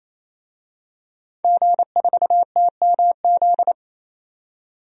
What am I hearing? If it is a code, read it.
Morse code, heard as a keyed tone, about 28 wpm, 700 Hz. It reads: G4TMZ
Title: DXCC-100 Call Signs - 5 Characters - 28wpm